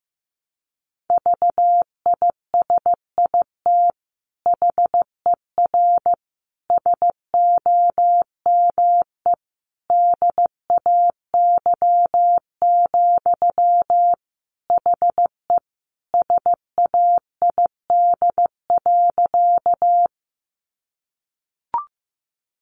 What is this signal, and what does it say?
Morse code, heard as a keyed tone, about 15 wpm, 700 Hz. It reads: VISIT HER SOME DAY, HE SAID.
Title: Sentences from Top 900 Words 15wpm